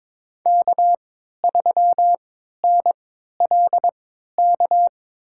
Morse code, heard as a keyed tone, about 22 wpm, spaced as for 13 wpm, 700 Hz. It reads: K3NLK
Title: USA Call Signs - 3x Character Spacing - 22wpm